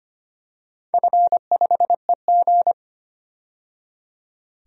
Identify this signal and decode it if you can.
Morse code, heard as a keyed tone, about 25 wpm, 700 Hz. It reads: F5EG
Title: DXCC-100 Call Signs - 4 Characters - 25wpm